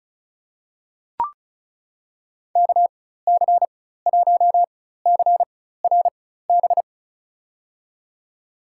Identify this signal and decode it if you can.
Morse code, heard as a keyed tone, about 34 wpm, spaced as for 16 wpm, 700 Hz. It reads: KC1CRB
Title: USA Call Signs - 6 Characters - 4x Character Spacing - 35wpm